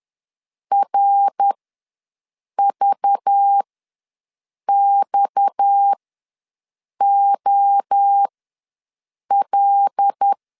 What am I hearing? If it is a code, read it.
Morse code, heard as a keyed tone, about 10 wpm, spaced as for 6 wpm, 800 Hz. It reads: RVXOL
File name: LEZIONI / 6 PKX0 / 6 PKX0 ws 10wpm 4220